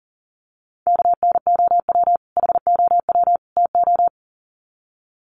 Morse code, heard as a keyed tone, about 40 wpm, 700 Hz. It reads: KNOW HOW TO
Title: Sets of 3 Words 40wpm